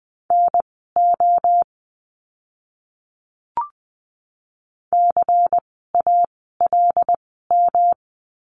Morse code, heard as a keyed tone, about 20 wpm, spaced as for 14 wpm, 700 Hz. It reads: NO CALM
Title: 108 Lesson 18f - Letter m - Words 20wpm